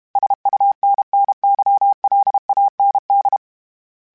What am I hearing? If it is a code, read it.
Morse code, heard as a keyed tone, about 32 wpm, 800 Hz. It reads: SUNNYLAND
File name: fromSMs / FromSMs mp3s / Cities@30